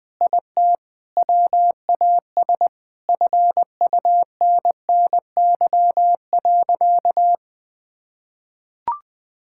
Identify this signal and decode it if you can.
Morse code, heard as a keyed tone, about 20 wpm, 700 Hz. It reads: IT WAS FUNNY.
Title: Sentences with 4-Word Phrases - 20wpm